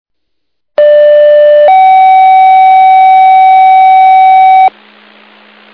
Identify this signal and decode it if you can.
Two-tone page, heard as a one-second tone, then a three-second tone, about 590 then 760 Hz, with noise. It is heard from 0.8 to 4.7 s.